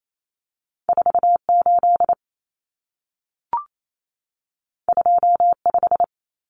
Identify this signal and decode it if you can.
Morse code, heard as a keyed tone, about 28 wpm, 700 Hz. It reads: 48 25
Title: Numbers - 2-Digit 28wpm